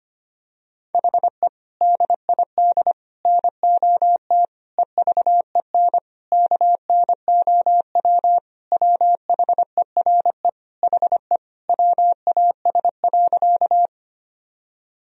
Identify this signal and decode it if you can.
Morse code, heard as a keyed tone, about 25 wpm, 700 Hz. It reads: HE DID NOT EVEN KNOW WHERE HE WAS.
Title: Sentences from Top 1000 Words - Review Part 1 - 25wpm